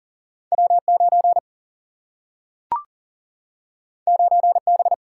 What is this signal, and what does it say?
Morse code, heard as a keyed tone, about 40 wpm, 700 Hz. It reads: W 9 9 B
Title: Single-Letter Single-Number 40wpm